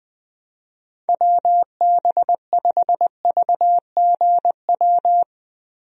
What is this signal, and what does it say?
Morse code, heard as a keyed tone, about 21 wpm, 700 Hz. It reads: WB5VGW
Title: USA Call Signs - 6 Characters - Repeated 3x - 20wpm